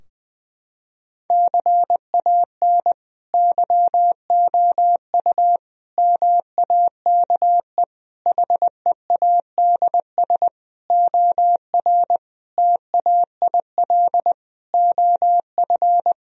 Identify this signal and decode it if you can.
Morse code, heard as a keyed tone, about 20 wpm, 700 Hz. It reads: CAN YOU MAKE HEADS OR TAIL OF
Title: Sentences from Top 1000 Words - Review Part 3 - 20wpm